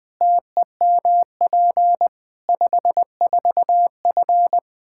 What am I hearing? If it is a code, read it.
Morse code, heard as a keyed tone, about 21 wpm, 700 Hz. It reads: TEMP 54F
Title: Practice QSO - Exchange 2 - Info - 20wpm